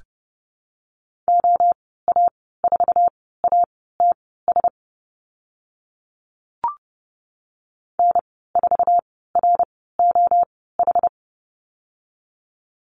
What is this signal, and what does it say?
Morse code, heard as a keyed tone, about 30 wpm, spaced as for 17 wpm, 700 Hz. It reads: OA4ATS N4ROH
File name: Lesson 15g - Number 5 - Call Signs 30wpm